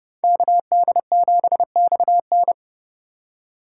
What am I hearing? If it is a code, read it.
Morse code, heard as a keyed tone, about 31 wpm, 700 Hz. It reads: KD7XN